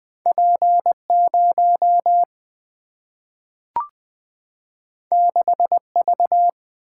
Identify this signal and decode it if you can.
Morse code, heard as a keyed tone, about 20 wpm, 700 Hz. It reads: P 0 6 V
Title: Single-Letter Single-Number 20wpm